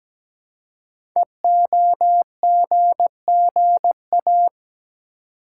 Morse code, heard as a keyed tone, about 17 wpm, 700 Hz. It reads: EOGGA
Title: Code Groups - 5 Letters - 17wpm